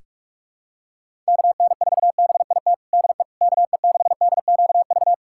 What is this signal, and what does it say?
Morse code, heard as a keyed tone, about 45 wpm, 700 Hz. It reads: KN4BIT DE KE6DQV
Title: USA Callsign DE Callsign - 6 Characters - 45wpm